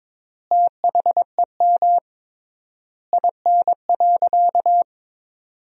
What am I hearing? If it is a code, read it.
Morse code, heard as a keyed tone, about 22 wpm, 700 Hz. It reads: THEM IN.
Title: 3x Word-Spacing - Sentences from Top 500 Words - 22wpm